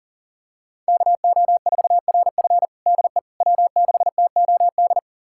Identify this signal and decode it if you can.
Morse code, heard as a keyed tone, about 40 wpm, 700 Hz. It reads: KO4RF DE W6TOD
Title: USA Callsign DE Callsign - 5 Characters - 40wpm